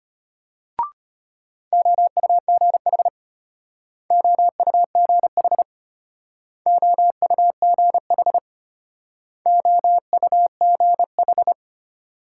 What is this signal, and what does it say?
Morse code, heard as a keyed tone, about 32 wpm, 700 Hz. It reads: OUGH OUGH OUGH OUGH